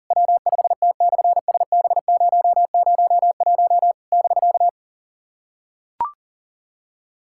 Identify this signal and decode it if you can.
Morse code, heard as a keyed tone, about 37 wpm, 700 Hz. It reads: W5T/SB001 <BK>
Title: SOTA QSO Practice 40wpm